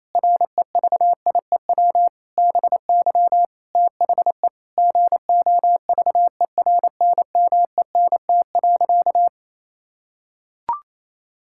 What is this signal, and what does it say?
Morse code, heard as a keyed tone, about 28 wpm, 700 Hz. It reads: REVIEW BY THE GOVERNMENT.
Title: Sentences from Top 1000 Words - Review Part 6 - 28wpm